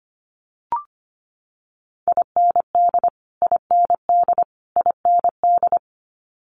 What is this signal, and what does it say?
Morse code, heard as a keyed tone, about 25 wpm, 700 Hz. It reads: IND IND IND
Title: Multi-letter Phonemes Repeated 3x 25wpm